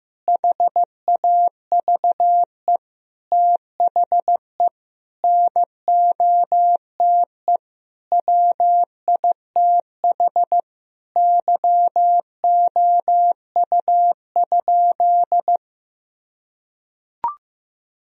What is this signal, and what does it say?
Morse code, heard as a keyed tone, about 15 wpm, 700 Hz. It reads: HAVE THE NOTE WITH YOU?
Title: Sentences from Top 1000 Words - Review Part 2 - 15wpm